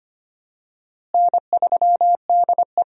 Morse code, heard as a keyed tone, about 25 wpm, 700 Hz. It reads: N3DE
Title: USA Call Signs - 4 Characters - Repeated 3x - 25wpm